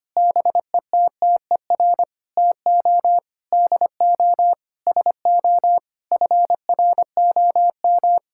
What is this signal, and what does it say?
Morse code, heard as a keyed tone, about 25 wpm, 700 Hz. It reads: BETTER TO DO SO FROM
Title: Sentences from Top 1600 Words 25wpm